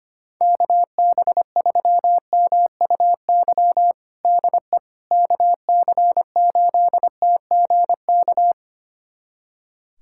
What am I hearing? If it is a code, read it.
Morse code, heard as a keyed tone, about 25 wpm, 700 Hz. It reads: KB3MUY DE KC8TGK
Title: USA Callsign DE Callsign - 6 Characters - 25wpm